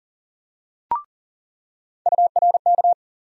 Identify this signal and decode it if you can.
Morse code, heard as a keyed tone, about 40 wpm, 700 Hz. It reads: URK
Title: USA Call Sign Postfixes - 3 Letters 40wpm